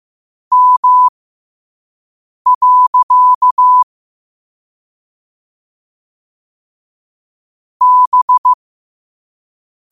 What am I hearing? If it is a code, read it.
Morse code, heard as a keyed tone, about 15 wpm, spaced as for 5 wpm, 1000 Hz. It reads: M. B